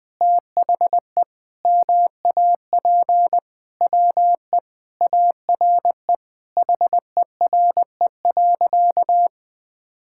Morse code, heard as a keyed tone, about 20 wpm, 700 Hz. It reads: THE MAP WE ARE HERE.
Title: Sentences with 5-Word Phrases - 20wpm